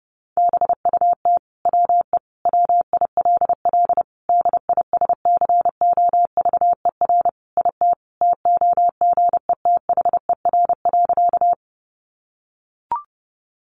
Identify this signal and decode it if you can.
Morse code, heard as a keyed tone, about 30 wpm, 700 Hz. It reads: BUT WE WILL DISCOVER IT TOGETHER.
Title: Sentences from Top 1100 Words 30wpm